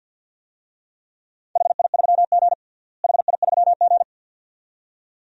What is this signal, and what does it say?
Morse code, heard as a keyed tone, about 52 wpm, 700 Hz. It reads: HI3G HI3G
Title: DXCC-100 Call Signs - 4 Characters - Repeated 2x - 50wpm